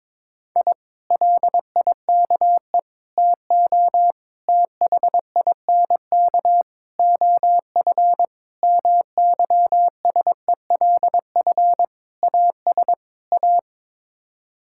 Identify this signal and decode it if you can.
Morse code, heard as a keyed tone, about 22 wpm, 700 Hz. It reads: I LIKE TO THINK OF MYSELF AS A
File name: practice / 22wpm / Sets of 8 Words - 22wpm